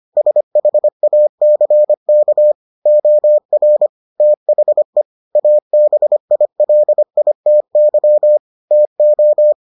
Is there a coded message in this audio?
Yes: Morse code from 0.2 to 9.5 s, heard as a keyed tone, about 25 wpm, 600 Hz. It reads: SHACK OR THE ABILITY TO